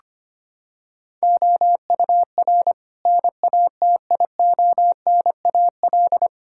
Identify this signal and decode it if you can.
Morse code, heard as a keyed tone, about 25 wpm, 700 Hz. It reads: OUR NATIONAL
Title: Sentences from Top 1000 Words - Review Part 6 - 25wpm